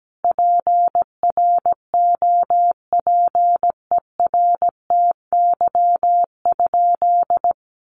Morse code, heard as a keyed tone, about 17 wpm, 700 Hz. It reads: PROPERTY?
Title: Sentences from Top 1000 Words 17wpm